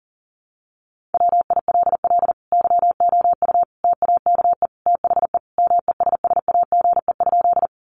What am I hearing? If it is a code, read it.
Morse code, heard as a keyed tone, about 40 wpm, 700 Hz. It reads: WILL YOU TAKE THE MESSAGE?
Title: Sentences from Top 1100 Words 40wpm